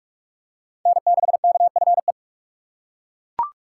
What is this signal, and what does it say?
Morse code, heard as a keyed tone, about 42 wpm, 700 Hz. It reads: N6KFE
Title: USA Call Signs - 5 Characters - 4x Character Spacing - 45wpm